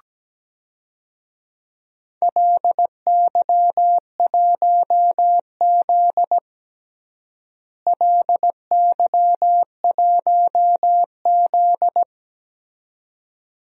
Morse code, heard as a keyed tone, about 17 wpm, 700 Hz. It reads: LY1Z LY1Z
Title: DXCC-100 Call Signs - 4 Characters - Repeated 2x - 17wpm